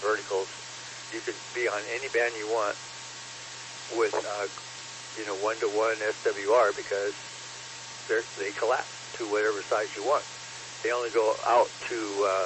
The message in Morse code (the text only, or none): none